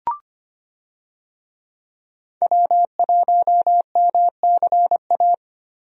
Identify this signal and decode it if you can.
Morse code, heard as a keyed tone, about 25 wpm, 690 Hz. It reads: W1MCA
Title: USA Call Signs - 5 Characters - 25wpm